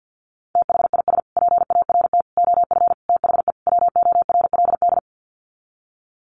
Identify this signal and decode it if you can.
Morse code, heard as a keyed tone, about 50 wpm, 700 Hz. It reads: THIS PART OF THE WORLD
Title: Sets of 5 Words Encore2 50wpm